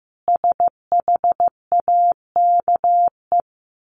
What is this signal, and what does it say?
Morse code, heard as a keyed tone, about 15 wpm, 700 Hz. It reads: SHAKE
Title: A Sound Of Thunder 15wpm